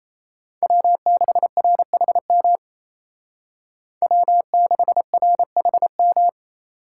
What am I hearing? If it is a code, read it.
Morse code, heard as a keyed tone, about 36 wpm, 700 Hz. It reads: W6RHM W6RHM